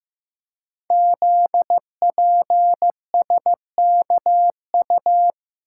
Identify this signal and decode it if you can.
Morse code, heard as a keyed tone, about 15 wpm, 700 Hz. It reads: ZPSKU
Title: Code Groups - 5 Letter:Numbers - 15wpm